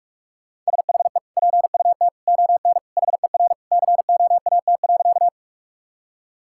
Morse code, heard as a keyed tone, about 46 wpm, 700 Hz. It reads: SHE PUT ON HER COAT.
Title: Sentences from Top 1600 Words 45wpm